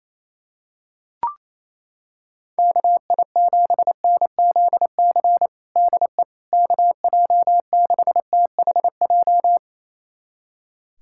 Morse code, heard as a keyed tone, about 28 wpm, 700 Hz. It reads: KI7NZC DE KJ6THJ
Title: USA Callsign DE Callsign - 6 Characters - 28wpm